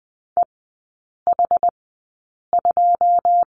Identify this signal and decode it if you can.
Morse code, heard as a keyed tone, about 19 wpm, spaced as for 8 wpm, 700 Hz. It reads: EH2
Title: 133 Lesson 22e - Question Mark - Three Characters 20wpm